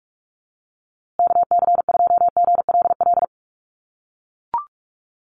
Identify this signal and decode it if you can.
Morse code, heard as a keyed tone, about 44 wpm, 700 Hz. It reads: KC2GLL
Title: USA Call Signs Repeated 3x 45wpm